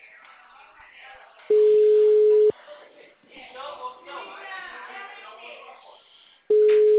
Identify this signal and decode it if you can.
Ringback tone (Europe, 425 Hz), with speech.